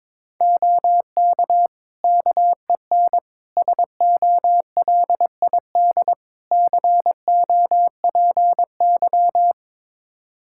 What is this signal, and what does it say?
Morse code, heard as a keyed tone, about 22 wpm, 700 Hz. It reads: OK KEN SOLID COPY
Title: Practice QSO - Exchange 3 - More Info - 22wpm